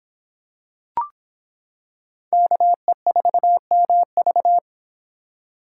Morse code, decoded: KE4MV

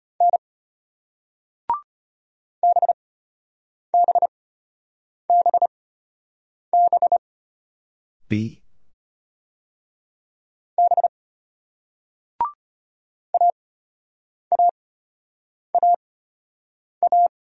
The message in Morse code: N BBBB B AAAA

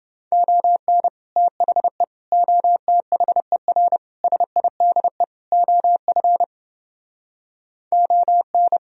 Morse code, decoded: ON THE OTHER SIDE OF ON

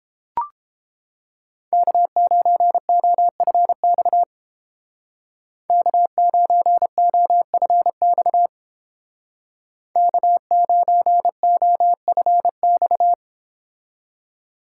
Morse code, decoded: K9OFX K9OFX K9OFX

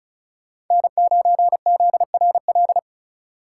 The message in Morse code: N9ZRL

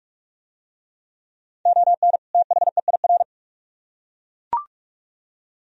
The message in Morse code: ON THEIR

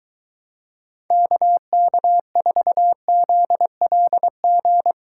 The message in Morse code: KK4ZLG

